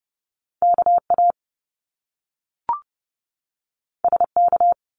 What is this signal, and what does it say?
Morse code, heard as a keyed tone, about 30 wpm, 700 Hz. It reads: KA SK
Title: Top Two Letter Combinations in Words 30wpm